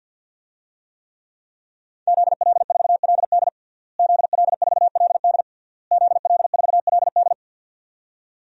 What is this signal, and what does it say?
Morse code, heard as a keyed tone, about 52 wpm, 700 Hz. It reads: ZL4LD ZL4LD ZL4LD
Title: DXCC-100 Call Signs - 5 Characters - Repeated 3x - 50wpm